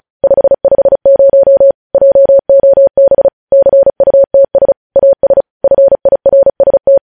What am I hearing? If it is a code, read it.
Morse code, heard as a keyed tone, about 36 wpm, 550 Hz. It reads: 550 JOB CUTS AS FIRST
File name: podcast / News headlines-1x-35x35